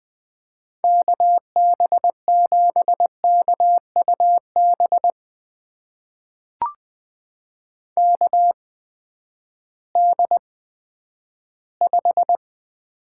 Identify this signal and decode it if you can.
Morse code, heard as a keyed tone, about 20 wpm, spaced as for 11 wpm, 700 Hz. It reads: KB7KUB KD5
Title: USA Call Signs - 6 Characters - 8x Character Spacing - 20wpm